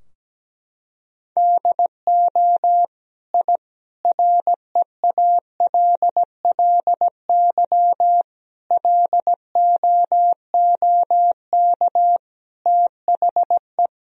Morse code, heard as a keyed tone, about 17 wpm, 700 Hz. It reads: DO I REALLY LOOK THE
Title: Sentences from Top 1000 Words - Review Part 2 - 17wpm